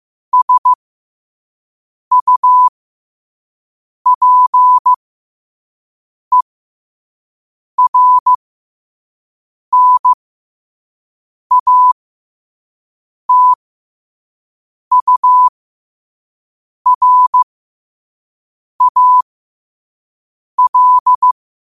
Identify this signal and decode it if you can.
Morse code, heard as a keyed tone, about 15 wpm, spaced as for 5 wpm, 1000 Hz. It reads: SUPERNATURAL
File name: 5wpm